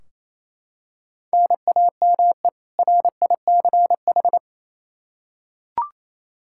Morse code, heard as a keyed tone, about 28 wpm, 700 Hz. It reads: NAME RICH
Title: K1USN SST QSO Practice 28wpm